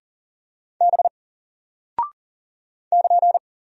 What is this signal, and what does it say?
Morse code, heard as a keyed tone, about 40 wpm, 700 Hz. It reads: B <KN>
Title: Single Character (Letter:Number:Punctuation:Prosign) Alt 40wpm